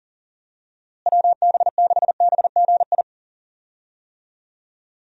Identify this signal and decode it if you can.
Morse code, heard as a keyed tone, about 40 wpm, 700 Hz. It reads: WB6BGI
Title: USA Call Signs - 6 Characters - Repeated 2x - 40wpm